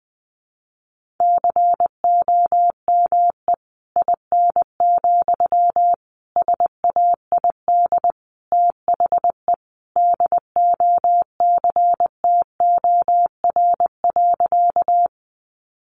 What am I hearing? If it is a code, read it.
Morse code, heard as a keyed tone, about 20 wpm, 700 Hz. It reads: COME IN, SAID THE DOCTOR.
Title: Sentences from Top 1500 Words 20wpm